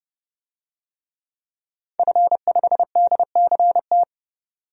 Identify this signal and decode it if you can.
Morse code, heard as a keyed tone, about 30 wpm, 700 Hz. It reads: F5DCT
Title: DXCC-100 Call Signs - 5 Characters - 30wpm